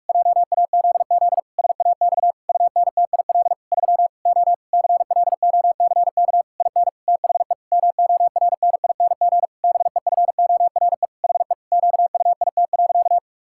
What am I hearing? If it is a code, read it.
Morse code, heard as a keyed tone, about 46 wpm, 700 Hz. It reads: JAZZ SAX UNTIL 3 O CLOCK IN THE MORNING BEFORE HE QUIT.
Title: Pangram Challenge v2 - 45wpm